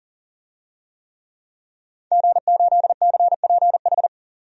Morse code, heard as a keyed tone, about 40 wpm, 700 Hz. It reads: G8CPH
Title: DXCC-100 Call Signs - 5 Characters - 40wpm